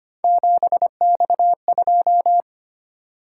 Morse code, heard as a keyed tone, about 25 wpm, 700 Hz. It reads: 7X2
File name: Three Letter-Number Combinations - 25wpm